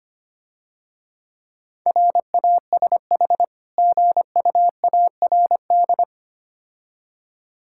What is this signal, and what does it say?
Morse code, heard as a keyed tone, about 26 wpm, 700 Hz. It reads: RASH GUARD
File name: practice / 25wpm / Difficult Words - Speed-Racing with Hints - 25wpm